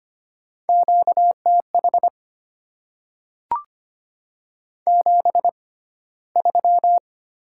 Morse code, heard as a keyed tone, about 25 wpm, spaced as for 15 wpm, 700 Hz. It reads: QTH 73
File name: Top 20 QSO Elements - 6x Character Spacing - 25wpm